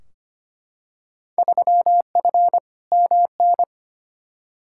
Morse code, heard as a keyed tone, about 25 wpm, 700 Hz. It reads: 3F MN
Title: Field Day Exchanges Encore - 25wpm